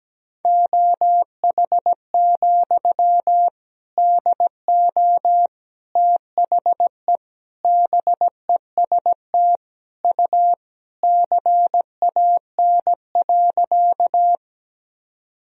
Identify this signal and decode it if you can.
Morse code, heard as a keyed tone, about 17 wpm, 700 Hz. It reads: OH, DO THE BEST U CAN.
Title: Sentences from Top 500 Words and Abbreviations 17wpm